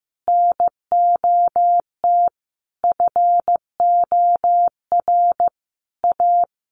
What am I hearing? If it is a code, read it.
Morse code, heard as a keyed tone, about 15 wpm, 700 Hz. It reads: NOT FOR A
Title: Sentences from Top 900 Words 15wpm